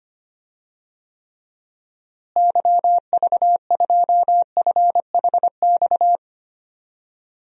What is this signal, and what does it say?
Morse code, heard as a keyed tone, about 25 wpm, 700 Hz. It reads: YV2FHX
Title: DXCC-100 Call Signs - 6 Characters - Repeated 2x - 25wpm